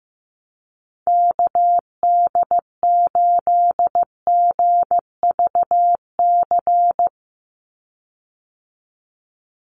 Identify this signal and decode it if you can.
Morse code, heard as a keyed tone, about 15 wpm, 700 Hz. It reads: KD8GVC
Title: USA Call Signs 15wpm